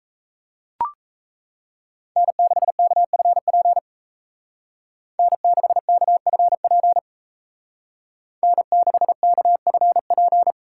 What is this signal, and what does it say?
Morse code, heard as a keyed tone, about 33 wpm, 700 Hz. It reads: N6KFP N6KFP N6KFP
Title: Speed Racing - USA Call Signs 28wpm